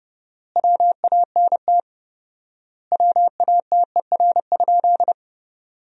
Morse code, heard as a keyed tone, about 32 wpm, 700 Hz. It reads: WANT WATER?
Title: 4x Word-Spacing - Sentences from Top 200 Words - 30wpm